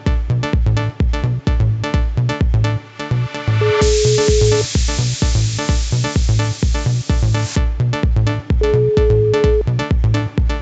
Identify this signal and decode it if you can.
Ringback tone (Europe, 425 Hz), with noise.